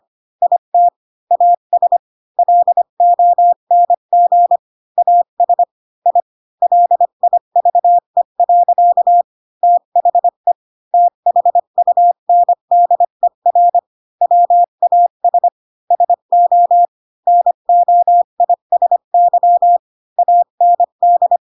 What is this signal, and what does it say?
Morse code, heard as a keyed tone, about 25 wpm, 700 Hz. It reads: IT AS LONG AS I LIVE. THE THUNDER WAS SO NOISY AND
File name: Pinocchio / Pinocchio ch7 20 25 ik1hge